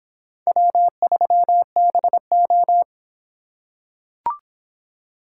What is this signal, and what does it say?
Morse code, heard as a keyed tone, about 26 wpm, 700 Hz. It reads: W3BO